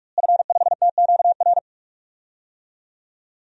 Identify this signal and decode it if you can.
Morse code, heard as a keyed tone, about 44 wpm, 700 Hz. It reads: F5TQR
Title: DXCC-100 Call Signs - 5 Characters - Repeated 3x - 45wpm